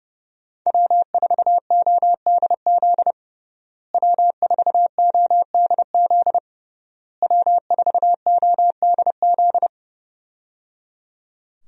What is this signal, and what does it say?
Morse code, heard as a keyed tone, about 30 wpm, 700 Hz. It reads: W4ODZ W4ODZ W4ODZ
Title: USA Call Signs - 5 Characters - Repeated 3x - 30wpm